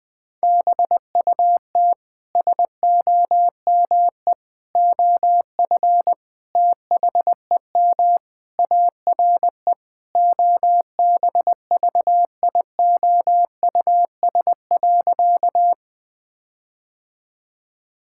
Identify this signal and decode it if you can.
Morse code, heard as a keyed tone, about 20 wpm, 700 Hz. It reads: BUT SOME OF THEM ARE OBVIOUS.
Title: Sentences from Top 1600 Words 20wpm